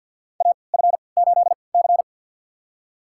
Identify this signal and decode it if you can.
Morse code, heard as a keyed tone, about 52 wpm, spaced as for 28 wpm, 700 Hz. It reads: AF8C
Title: USA Call Signs - 3x Character Spacing - 50wpm